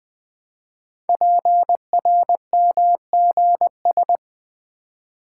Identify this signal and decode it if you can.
Morse code, heard as a keyed tone, about 20 wpm, 700 Hz. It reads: PRMGS
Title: Code Groups - 5 Letters - 20wpm